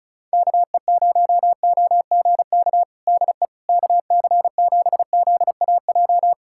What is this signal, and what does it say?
Morse code, heard as a keyed tone, about 36 wpm, 700 Hz. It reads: KE0OGK DE KC7ZAJ
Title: USA Callsign DE Callsign - 6 Characters - 35wpm